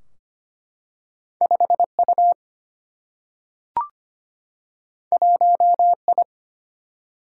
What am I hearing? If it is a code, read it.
Morse code, heard as a keyed tone, about 25 wpm, 700 Hz. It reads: 5 U 1 I
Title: Single-Letter Single-Number 25wpm